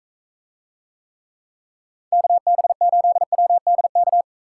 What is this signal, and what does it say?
Morse code, heard as a keyed tone, about 43 wpm, 700 Hz. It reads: KB8WDK